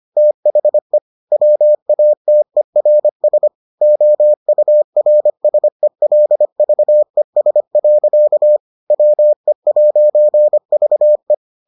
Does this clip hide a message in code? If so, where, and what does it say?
Yes: Morse code from 0.2 to 11.3 s, heard as a keyed tone, about 25 wpm, 600 Hz. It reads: THE WATERS OURSELVES. WE'VE